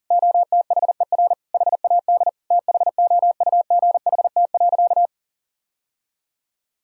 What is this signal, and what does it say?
Morse code, heard as a keyed tone, about 40 wpm, 700 Hz. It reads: OTHER HAD THOUGHT.
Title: Sentences from Top 1000 Words - Review Part 1 - 40wpm